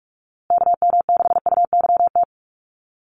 Morse code, heard as a keyed tone, about 44 wpm, 700 Hz. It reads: KM6UYT
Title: USA Call Signs Repeated 2x 45wpm